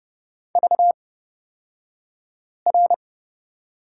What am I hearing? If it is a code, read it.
Morse code, heard as a keyed tone, about 30 wpm, spaced as for 5 wpm, 700 Hz. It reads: VR
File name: Lesson 36d - BT - Two Characters 30wpm